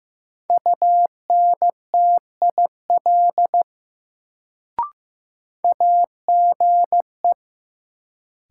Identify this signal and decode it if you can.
Morse code, heard as a keyed tone, about 15 wpm, 700 Hz. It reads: UNTIL AGE
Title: Sentences from Top 500 Words - New Word Drill - 15wpm